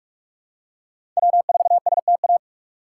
Morse code, heard as a keyed tone, about 48 wpm, 690 Hz. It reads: W4STA
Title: USA Call Signs - 5 Characters - 2x Character Spacing - 45wpm